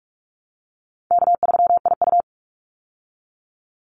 Morse code, heard as a keyed tone, about 44 wpm, 700 Hz. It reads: K3IU
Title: Easy Call Signs Repeated 3x - 45wpm